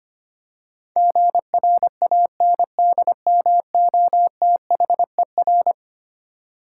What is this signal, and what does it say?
Morse code, heard as a keyed tone, about 25 wpm, 700 Hz. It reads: GRANDMOTHER
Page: morse-code-ninja-practice-25wpm